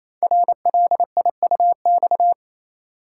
Morse code, heard as a keyed tone, about 29 wpm, 700 Hz. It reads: RLIUX
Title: Code Groups - 5 Letter:Numbers - 28wpm